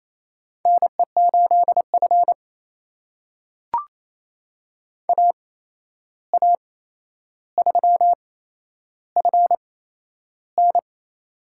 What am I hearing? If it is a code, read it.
Morse code, heard as a keyed tone, about 27 wpm, spaced as for 11 wpm, 700 Hz. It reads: NE8F AA3FN
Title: USA Call Signs - 8x Character Spacing - 28wpm